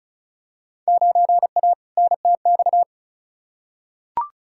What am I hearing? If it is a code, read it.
Morse code, heard as a keyed tone, about 36 wpm, 690 Hz. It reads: 9A NTX
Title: Field Day Exchanges Encore - 35wpm